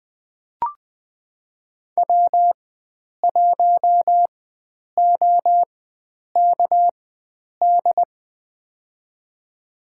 Morse code, heard as a keyed tone, about 20 wpm, spaced as for 9 wpm, 700 Hz. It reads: W1OKD
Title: USA Call Signs - 4x Character Spacing - 20wpm